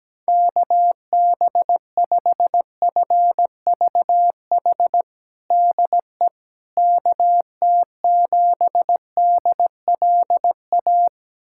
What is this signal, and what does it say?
Morse code, heard as a keyed tone, about 17 wpm, 700 Hz. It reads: KB5FVH DE KT7DLA
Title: USA Callsign DE Callsign 17wpm